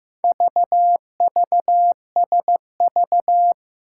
Morse code, heard as a keyed tone, about 15 wpm, 700 Hz. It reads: VVSV
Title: Four Challenging Letter-Number Combinations 15wpm